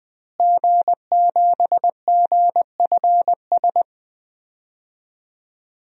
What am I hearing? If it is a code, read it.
Morse code, heard as a keyed tone, about 20 wpm, 700 Hz. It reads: G7GFS
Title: DXCC-100 Call Signs - 5 Characters - Repeated 2x - 20wpm